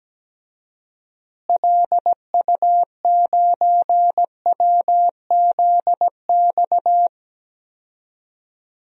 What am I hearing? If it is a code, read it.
Morse code, heard as a keyed tone, about 17 wpm, 700 Hz. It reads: LU9WZX
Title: DXCC-100 Call Signs - 6 Characters - Repeated 3x - 17wpm